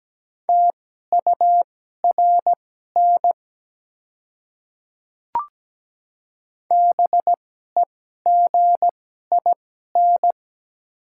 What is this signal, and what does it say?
Morse code, heard as a keyed tone, about 17 wpm, spaced as for 12 wpm, 700 Hz. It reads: TURN BEGIN
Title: 2x Character Spacing - Top 300 Words - 17wpm